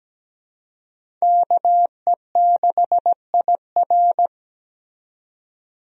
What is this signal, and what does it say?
Morse code, heard as a keyed tone, about 17 wpm, 700 Hz. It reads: KE6IR
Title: USA Call Signs - 5 Characters - 17wpm